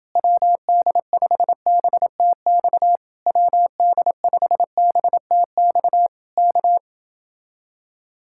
Morse code, heard as a keyed tone, about 28 wpm, 700 Hz. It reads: WD5BTX WD5BTX K